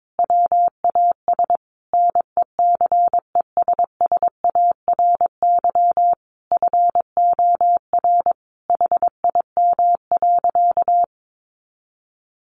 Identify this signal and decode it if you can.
Morse code, heard as a keyed tone, about 22 wpm, 700 Hz. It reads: WAS NECESSARY FOR HIM.
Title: Sentences from Top 1500 Words 22wpm